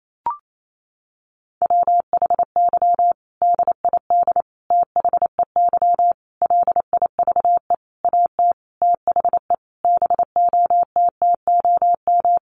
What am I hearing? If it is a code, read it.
Morse code, heard as a keyed tone, about 28 wpm, 700 Hz. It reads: WHY DID THEY LIVE AT THE BOTTOM
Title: Sentences from Top 900 Words 28wpm